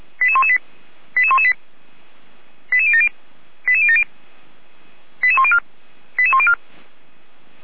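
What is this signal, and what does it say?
Five-tone selective call (ZVEI), heard as quick runs of five tones, tones of about 70 ms. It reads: 88118 88118 88070 88070 88104 88104
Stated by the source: ZVEI five-tone selective call received with an SDR, from the Feuerwehr Orlamünde fire brigade audio server.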